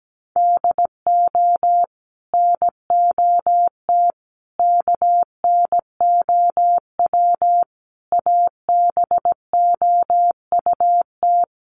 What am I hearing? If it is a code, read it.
Morse code, heard as a keyed tone, about 17 wpm, 700 Hz. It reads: DO NOT KNOW ABOUT
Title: Sets of 5 Words Encore2 17wpm